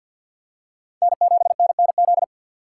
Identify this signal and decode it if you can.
Morse code, heard as a keyed tone, about 53 wpm, 700 Hz. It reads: N7NNZ
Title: USA Call Signs - 5 Characters - 2x Character Spacing - 50wpm